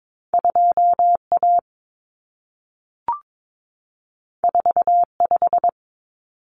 Morse code, heard as a keyed tone, about 22 wpm, 700 Hz. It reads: 2A 45
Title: Common Cut Numbers - 2-Digit 22wpm